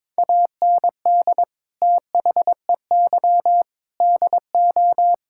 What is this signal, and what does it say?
Morse code, heard as a keyed tone, about 22 wpm, 700 Hz. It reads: AND THEY DO
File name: practice / 22wpm / Sentences with 4-Word Phrases - 22wpm